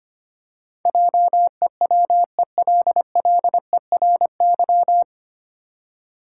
Morse code, heard as a keyed tone, about 25 wpm, 700 Hz. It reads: JEWELLERY
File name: practice / 25wpm / 9 Letter Words - First 3 Letters Given - 25wpm